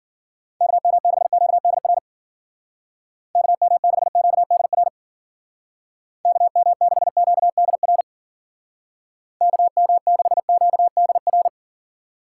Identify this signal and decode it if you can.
Morse code, heard as a keyed tone, about 50 wpm, 700 Hz. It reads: KM6QDR KM6QDR KM6QDR KM6QDR
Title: Speed Racing - USA 2x3 Call Signs 40wpm